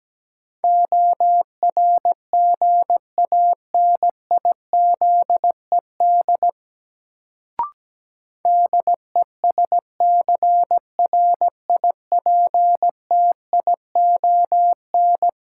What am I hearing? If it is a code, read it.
Morse code, heard as a keyed tone, about 17 wpm, 700 Hz. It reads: ORGANIZED DESCRIPTION
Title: Sentences from Top 1300 Words - New Word Drill - 17wpm